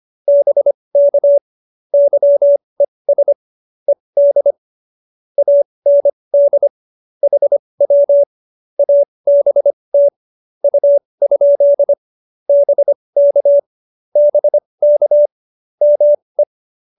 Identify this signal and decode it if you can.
Morse code, heard as a keyed tone, about 25 wpm, spaced as for 20 wpm, 590 Hz. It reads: BK YES ED AND HW ABT U? BK BK ME
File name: QSO316 20